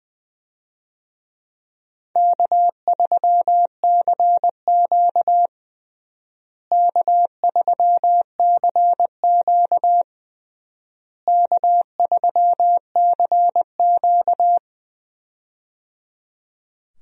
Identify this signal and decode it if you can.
Morse code, heard as a keyed tone, about 21 wpm, 700 Hz. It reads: K3CQ K3CQ K3CQ
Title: USA Call Signs - 4 Characters - Repeated 3x - 20wpm